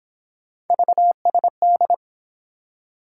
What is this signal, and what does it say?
Morse code, heard as a keyed tone, about 29 wpm, 700 Hz. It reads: VSD